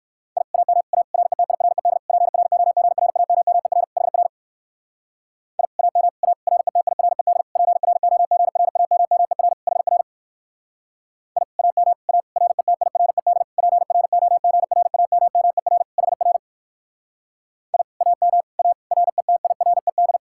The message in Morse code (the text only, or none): I AM A RETIRED PROGRAMMER HR I AM A RETIRED PROGRAMMER HR I AM A RETIRED PROGRAMMER HR I AM A RETIRED